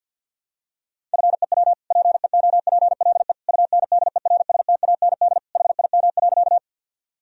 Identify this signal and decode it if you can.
Morse code, heard as a keyed tone, about 51 wpm, 700 Hz. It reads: FEW PEOPLE UNDERSTAND HIM.
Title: Sentences from Top 1000 Words - Review Part 1 - 50wpm